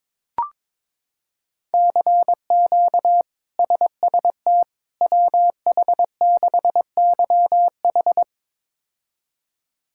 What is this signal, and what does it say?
Morse code, heard as a keyed tone, about 22 wpm, 700 Hz. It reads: CQ SST WH6YH
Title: K1USN SST QSO Practice - 22wpm